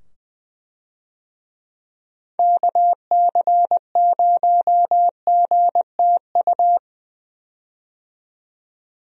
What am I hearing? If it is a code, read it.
Morse code, heard as a keyed tone, about 20 wpm, 700 Hz. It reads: KC0GTU
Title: USA Call Signs - 6 Characters - 20wpm